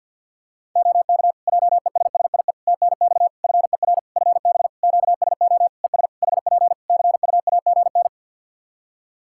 Morse code, heard as a keyed tone, about 52 wpm, 700 Hz. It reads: OK JESSIE TNX FER FB QSO ES HP CUAGN
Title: Practice QSO - Exchange 4 - Ending - 50wpm